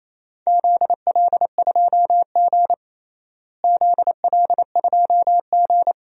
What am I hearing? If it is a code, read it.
Morse code, heard as a keyed tone, about 28 wpm, 700 Hz. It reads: ZL2G ZL2G
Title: DXCC-100 Call Signs - 4 Characters - Repeated 2x - 28wpm